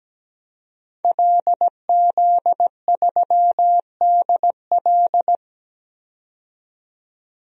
Morse code, heard as a keyed tone, about 17 wpm, 700 Hz. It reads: LZ3DL
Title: DXCC-100 Call Signs - 5 Characters - 17wpm